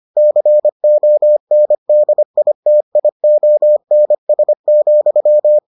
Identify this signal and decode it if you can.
Morse code, heard as a keyed tone, about 25 wpm, 600 Hz. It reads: CONDITIONS,